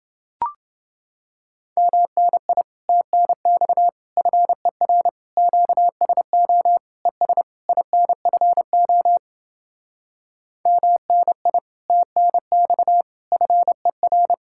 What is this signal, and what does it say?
Morse code, heard as a keyed tone, about 30 wpm, 700 Hz. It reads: MNI TNX FER QSO ES INFO MNI TNX FER